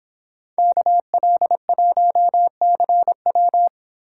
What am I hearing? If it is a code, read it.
Morse code, heard as a keyed tone, about 29 wpm, 700 Hz. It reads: KL1CW